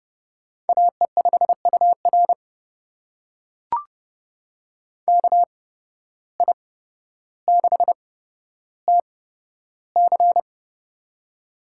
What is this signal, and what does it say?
Morse code, heard as a keyed tone, about 30 wpm, spaced as for 13 wpm, 700 Hz. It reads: AE5UR KI6TC
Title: USA Call Signs - 5 Characters - 8x Character Spacing - 30wpm